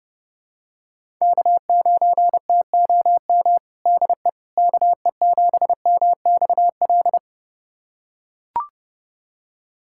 Morse code, heard as a keyed tone, about 32 wpm, 700 Hz. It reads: K9TOM DE KE7MXL
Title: USA Callsign DE Callsign 30wpm